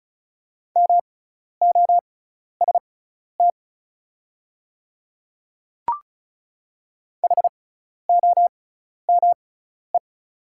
Morse code, heard as a keyed tone, about 36 wpm, spaced as for 12 wpm, 690 Hz. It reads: MOST HOME
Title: 6x Character Spacing - Top 200 Words - 35wpm